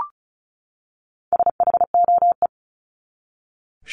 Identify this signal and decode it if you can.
Morse code, heard as a keyed tone, about 34 wpm, 700 Hz. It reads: SHOE
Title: Top 1751-2000 Words 35wpm